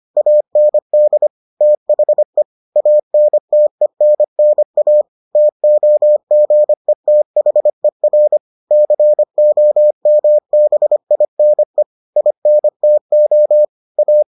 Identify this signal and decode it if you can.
Morse code, heard as a keyed tone, about 25 wpm, 600 Hz. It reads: AND THE ANTENNA TOGETHER COMBINE INTO A